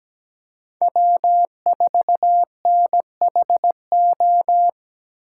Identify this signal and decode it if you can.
Morse code, heard as a keyed tone, about 17 wpm, 700 Hz. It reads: W4NHO
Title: USA Call Signs - 8x Character Spacing - 17wpm